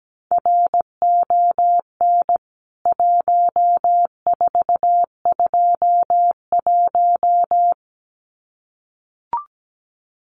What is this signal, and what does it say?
Morse code, heard as a keyed tone, about 17 wpm, 700 Hz. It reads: RON 1421
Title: CWT Exchange 17wpm